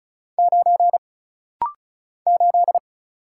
Morse code, heard as a keyed tone, about 35 wpm, 700 Hz. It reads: 9 8
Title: Single Letter-Number - Rapid-Fire 35wpm